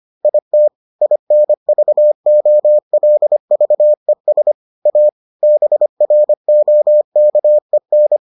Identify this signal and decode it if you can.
Morse code, heard as a keyed tone, about 25 wpm, 600 Hz. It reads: IT INVOLVES A BROKEN